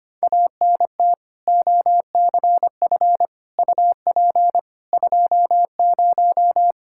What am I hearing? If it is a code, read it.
Morse code, heard as a keyed tone, about 25 wpm, 700 Hz. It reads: ANT OCF UP 20
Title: K1USN SST QSO Practice 25wpm